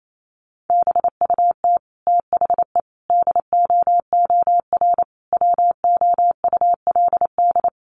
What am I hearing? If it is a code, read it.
Morse code, heard as a keyed tone, about 28 wpm, 700 Hz. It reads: BUT THE DOOR WOULD
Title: Sentences from Top 600 Words 28wpm